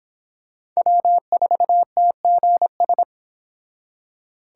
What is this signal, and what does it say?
Morse code, heard as a keyed tone, about 25 wpm, 700 Hz. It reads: W4TGS